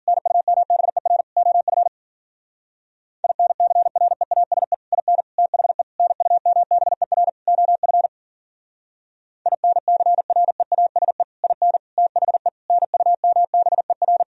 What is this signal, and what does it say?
Morse code, heard as a keyed tone, about 53 wpm, 700 Hz. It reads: NUMBER OF INCREASE IN THE NUMBER OF INCREASE IN THE NUMBER